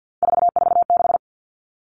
Morse code, heard as a keyed tone, about 50 wpm, 700 Hz. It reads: 446
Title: Numbers - 3-Digit 50wpm